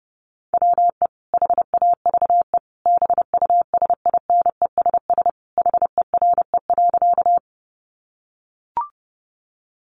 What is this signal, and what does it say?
Morse code, heard as a keyed tone, about 30 wpm, 700 Hz. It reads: WE HAVE BUSINESS HERE.
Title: Sentences from Top 600 Words 30wpm